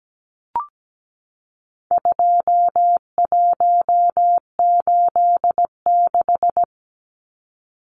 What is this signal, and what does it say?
Morse code, heard as a keyed tone, about 17 wpm, 700 Hz. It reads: 2186
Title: Numbers - 4-Digit 17wpm